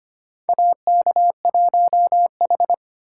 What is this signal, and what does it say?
Morse code, heard as a keyed tone, about 25 wpm, 700 Hz. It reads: AK1H
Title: USA Call Signs - 4 Characters - 2x Character Spacing - 25wpm